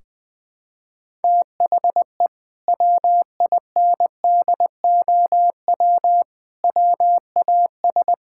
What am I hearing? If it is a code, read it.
Morse code, heard as a keyed tone, about 20 wpm, 700 Hz. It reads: THE WINDOW WAS
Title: Sentences from Top 1600 Words 20wpm